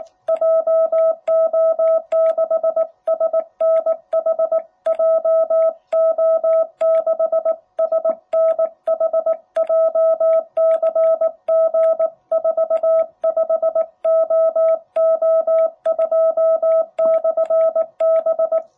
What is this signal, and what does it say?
Morse code, heard as a keyed tone, about 19 wpm, 660 Hz. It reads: JO6SNH JO6SNH JCG45OO2/B
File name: jh8cbh / 2021cw